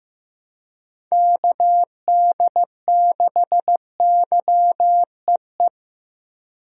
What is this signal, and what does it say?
Morse code, heard as a keyed tone, about 15 wpm, 700 Hz. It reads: KD6YEE
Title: USA Call Signs - 6 Characters - Repeated 3x - 15wpm